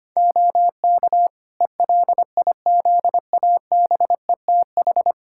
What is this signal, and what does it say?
Morse code, heard as a keyed tone, about 25 wpm, 700 Hz. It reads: OK ELIZABETH
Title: Practice QSO - Exchange 4 - Ending - 25wpm